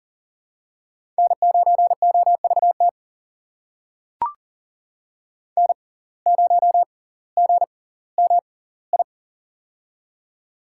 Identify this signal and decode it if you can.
Morse code, heard as a keyed tone, about 38 wpm, spaced as for 20 wpm, 700 Hz. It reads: N9OVT N0GMI
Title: USA Call Signs - 5 Characters - 6x Character Spacing - 40wpm